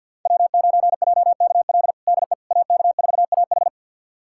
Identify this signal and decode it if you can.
Morse code, heard as a keyed tone, about 47 wpm, 700 Hz. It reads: W9JKL DE AK4AH
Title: USA Callsign DE Callsign 50wpm